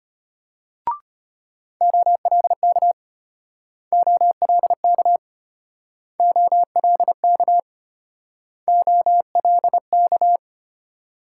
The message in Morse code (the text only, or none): OLK OLK OLK OLK